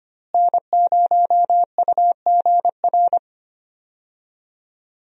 Morse code, heard as a keyed tone, about 25 wpm, 700 Hz. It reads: N0UGR
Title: USA Call Signs - 5 Characters - Repeated 3x - 25wpm